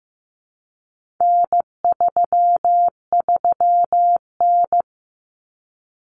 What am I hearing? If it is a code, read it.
Morse code, heard as a keyed tone, about 15 wpm, 700 Hz. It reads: N33N
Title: Common Cut Numbers - 4-Digit 15wpm